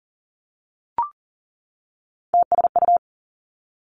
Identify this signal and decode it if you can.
Morse code, heard as a keyed tone, about 41 wpm, 700 Hz. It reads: TSU
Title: Three Challenging Letter-Number Combinations 40wpm